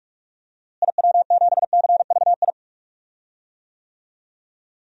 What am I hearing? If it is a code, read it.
Morse code, heard as a keyed tone, about 41 wpm, 700 Hz. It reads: IW7CVI
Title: DXCC-100 Call Signs - 6 Characters - Repeated 3x - 45wpm